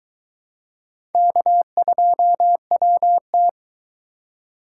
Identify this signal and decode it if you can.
Morse code, heard as a keyed tone, about 22 wpm, 700 Hz. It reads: K2WT